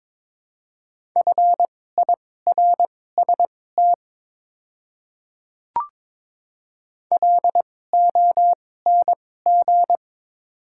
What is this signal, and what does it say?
Morse code, heard as a keyed tone, about 22 wpm, spaced as for 16 wpm, 700 Hz. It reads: FIRST LONG
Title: 2x Character Spacing - Top 100 Words - 22wpm